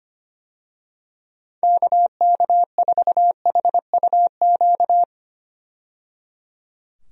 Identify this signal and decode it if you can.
Morse code, heard as a keyed tone, about 25 wpm, 700 Hz. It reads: KK4HUQ
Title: USA Call Signs - 6 Characters - 25wpm